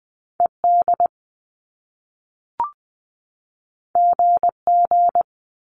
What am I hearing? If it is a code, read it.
Morse code, heard as a keyed tone, about 20 wpm, 700 Hz. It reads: ED GG
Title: Top Two Letter Combinations in Words 20wpm